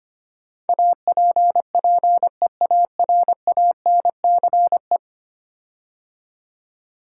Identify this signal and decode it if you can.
Morse code, heard as a keyed tone, about 25 wpm, 700 Hz. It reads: APPEARANCE
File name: 10 Letter Words - First 3 Letters Given - 25wpm